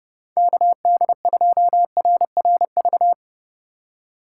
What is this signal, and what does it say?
Morse code, heard as a keyed tone, about 30 wpm, 700 Hz. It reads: KD2RRV
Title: USA Call Signs - 6 Characters - 2x Character Spacing - 30wpm